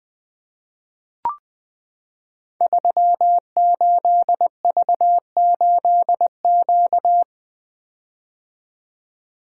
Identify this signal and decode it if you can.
Morse code, heard as a keyed tone, about 20 wpm, 700 Hz. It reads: 38V8Q
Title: Code Groups - 5 Letter:Numbers - 20wpm